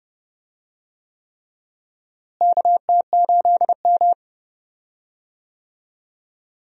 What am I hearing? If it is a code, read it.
Morse code, heard as a keyed tone, about 29 wpm, 700 Hz. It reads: KT8M
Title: USA Call Signs - 4 Characters - Repeated 3x - 30wpm